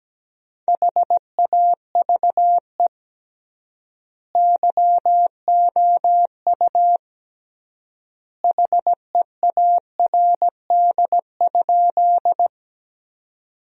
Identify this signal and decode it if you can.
Morse code, heard as a keyed tone, about 17 wpm, 700 Hz. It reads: HAVE YOU HEARD?
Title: 3x Word-Spacing - Sentences from Top 400 Words - 17wpm